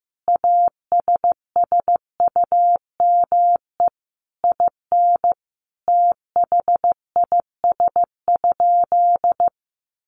Morse code, heard as a keyed tone, about 15 wpm, 700 Hz. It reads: ASSUME IN THIS?
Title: Sentences from Top 1500 Words 15wpm